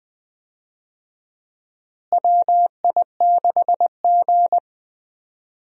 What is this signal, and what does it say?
Morse code, heard as a keyed tone, about 20 wpm, 700 Hz. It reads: WI6G
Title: USA Call Signs - 4 Characters - Repeated 3x - 20wpm